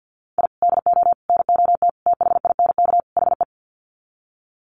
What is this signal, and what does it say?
Morse code, heard as a keyed tone, about 50 wpm, 700 Hz. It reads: I DO NOT THINK HE
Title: Sets of 5 Words 50wpm